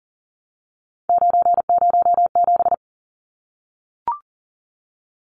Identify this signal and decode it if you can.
Morse code, heard as a keyed tone, about 40 wpm, 700 Hz. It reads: 907
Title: Numbers - 3-Digit 40wpm